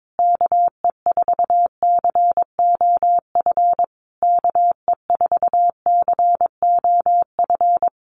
Morse code, heard as a keyed tone, about 22 wpm, 700 Hz. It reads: KE4COF KE4COF
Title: USA Call Signs Repeated 3x 22wpm